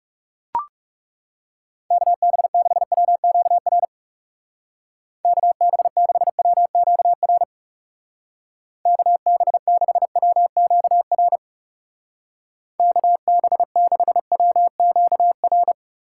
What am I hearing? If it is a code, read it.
Morse code, heard as a keyed tone, about 38 wpm, 700 Hz. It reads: KB6WQR KB6WQR KB6WQR KB6WQR